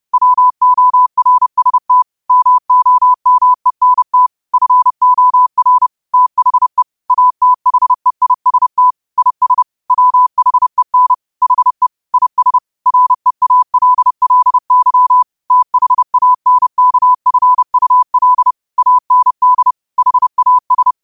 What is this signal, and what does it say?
Morse code, heard as a keyed tone, about 30 wpm, 1000 Hz. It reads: WORST MOMENT FOR THE ATHEIST IS WHEN HE IS REALLY THANKFUL AND HAS
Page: Quotes for Fri, 15 Aug 2025 in Morse Code at 30 words per minute.